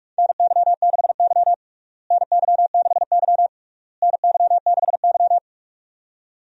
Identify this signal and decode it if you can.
Morse code, heard as a keyed tone, about 47 wpm, 700 Hz. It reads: NY6Y NY6Y NY6Y
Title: USA Call Signs - 4 Characters - Repeated 3x - 45wpm